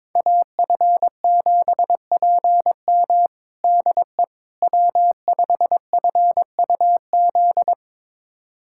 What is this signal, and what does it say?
Morse code, heard as a keyed tone, about 22 wpm, 700 Hz. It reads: AF7PM DE W5FUZ
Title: USA Callsign DE Callsign - 5 Characters - 22wpm